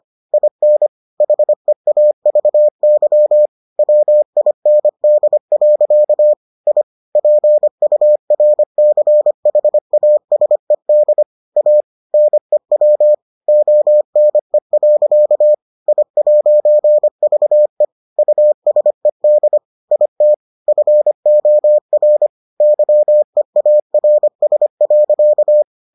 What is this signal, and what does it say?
Morse code, heard as a keyed tone, about 25 wpm, 600 Hz. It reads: IN HEAVY WIND. I PURCHASED A NEW ONE. I'VE USED IT FOR YEARS.